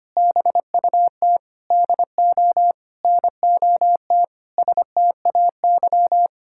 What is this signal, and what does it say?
Morse code, heard as a keyed tone, about 25 wpm, 700 Hz. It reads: BUT DO NOT STAY
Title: Sentences from Top 1600 Words 25wpm